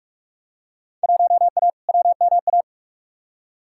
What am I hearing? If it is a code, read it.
Morse code, heard as a keyed tone, about 46 wpm, 700 Hz. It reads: 1A WMA
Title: Field Day Exchanges - 45wpm